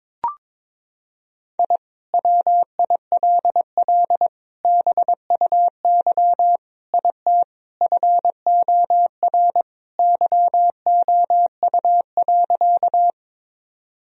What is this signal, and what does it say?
Morse code, heard as a keyed tone, about 23 wpm, 700 Hz. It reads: I WILL BUY IT FOR YOU.
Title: Sentences from Top 1000 Words - Review Part 4 - 22wpm